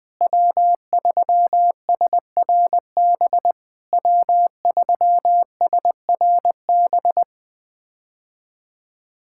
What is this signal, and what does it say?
Morse code, heard as a keyed tone, about 20 wpm, 700 Hz. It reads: W3SRB W3SRB
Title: POTA QSO Practice - 20wpm